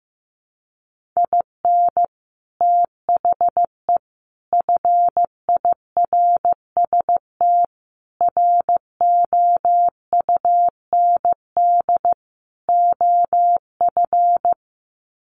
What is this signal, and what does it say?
Morse code, heard as a keyed tone, about 15 wpm, 700 Hz. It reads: IN THE FIRST ROUND OF
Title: Sets of 5 Words Encore 15wpm